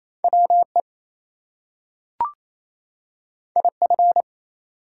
Morse code, heard as a keyed tone, about 28 wpm, 700 Hz. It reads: WE IF
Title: Two Letter Words - 28wpm